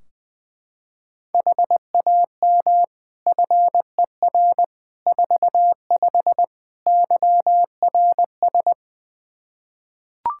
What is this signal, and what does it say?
Morse code, heard as a keyed tone, about 21 wpm, 700 Hz. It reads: HAM FER 45 YRS
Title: Practice QSO - Exchange 3 - More Info - 20wpm